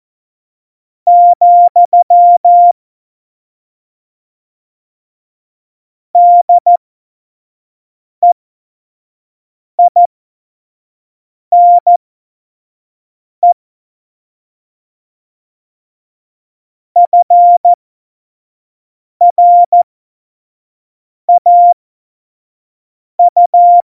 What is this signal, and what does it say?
Morse code, heard as a keyed tone, about 14 wpm, spaced as for 5 wpm, 700 Hz. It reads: , DEINE FRAU
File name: Morsekurs / Witze / Witz 22. Mai 2018, 25 F70